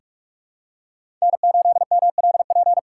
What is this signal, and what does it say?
Morse code, heard as a keyed tone, about 44 wpm, 700 Hz. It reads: N8MLP
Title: USA Call Signs - 5 Characters - 4x Character Spacing - 45wpm